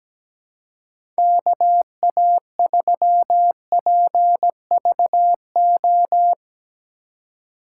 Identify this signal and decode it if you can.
Morse code, heard as a keyed tone, about 17 wpm, 700 Hz. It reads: KA3PVO
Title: USA Call Signs - 6 Characters - 2x Character Spacing - 17wpm